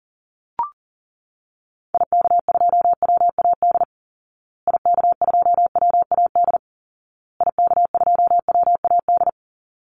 Morse code, heard as a keyed tone, about 37 wpm, 700 Hz. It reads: IK2WAD IK2WAD IK2WAD
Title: Call Signs Repeated 2x - 40wpm